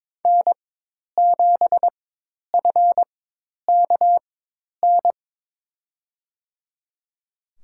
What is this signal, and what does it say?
Morse code, heard as a keyed tone, about 22 wpm, spaced as for 10 wpm, 700 Hz. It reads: N7FKN
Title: USA Call Signs - 5 Characters - 4x Character Spacing - 22wpm